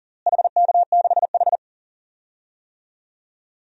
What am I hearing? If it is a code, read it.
Morse code, heard as a keyed tone, about 40 wpm, 700 Hz. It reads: HK6H
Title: DXCC-100 Call Signs - 4 Characters - Repeated 2x - 40wpm